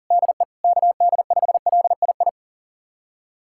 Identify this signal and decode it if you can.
Morse code, heard as a keyed tone, about 40 wpm, 700 Hz. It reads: DE KD5LII
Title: USA Callsign DE Callsign 40wpm